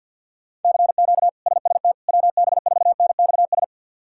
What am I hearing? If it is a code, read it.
Morse code, heard as a keyed tone, about 51 wpm, 700 Hz. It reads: CQ SST WB4NXS
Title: K1USN SST QSO Practice - 50wpm